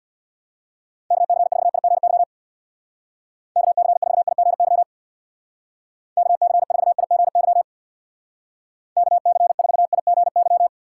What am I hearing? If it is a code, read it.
Morse code, heard as a keyed tone, about 61 wpm, 700 Hz. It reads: KC4IGY KC4IGY KC4IGY KC4IGY
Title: Speed Racing - USA Call Signs 50wpm